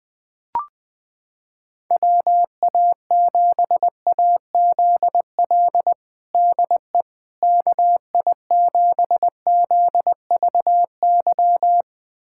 Morse code, heard as a keyed tone, about 20 wpm, 700 Hz. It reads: WA7AZL DE KI7ZVY
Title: USA Callsign DE Callsign - 6 Characters - 20wpm